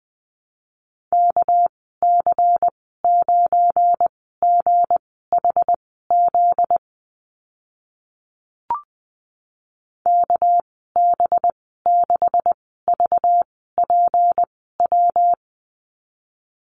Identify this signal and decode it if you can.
Morse code, heard as a keyed tone, about 20 wpm, spaced as for 14 wpm, 700 Hz. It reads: KC9GHZ KB6VPW
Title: 245 Lesson 39g - Letter z - Call Signs 20wpm